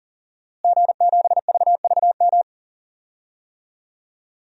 Morse code, heard as a keyed tone, about 40 wpm, 700 Hz. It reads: G7VVM
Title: DXCC-100 Call Signs - 5 Characters - Repeated 3x - 40wpm